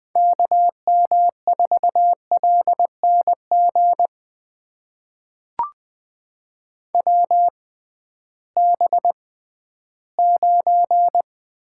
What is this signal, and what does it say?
Morse code, heard as a keyed tone, about 20 wpm, spaced as for 13 wpm, 700 Hz. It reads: KM4LNG WB9
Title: USA Call Signs - 6 Characters - 6x Character Spacing - 20wpm